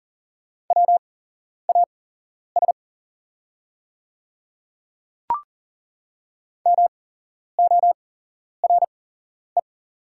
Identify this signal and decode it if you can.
Morse code, heard as a keyed tone, about 40 wpm, spaced as for 11 wpm, 700 Hz. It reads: WAS MORE
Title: 8x Character Spacing - Top 100 Words - 40wpm